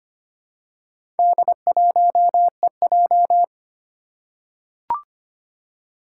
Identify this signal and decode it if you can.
Morse code, heard as a keyed tone, about 25 wpm, 700 Hz. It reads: D1EJ
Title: Four Challenging Letter-Number Combinations 25wpm